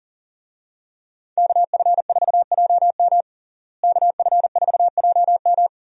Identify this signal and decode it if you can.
Morse code, heard as a keyed tone, about 40 wpm, 700 Hz. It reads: KF4JM KF4JM
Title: USA Call Signs - 5 Characters - Repeated 3x - 40wpm